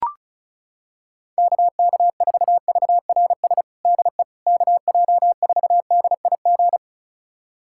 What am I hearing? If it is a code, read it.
Morse code, heard as a keyed tone, about 34 wpm, 700 Hz. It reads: KK4VRS DE KJ4DIG
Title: USA Callsign DE Callsign - 6 Characters - 35wpm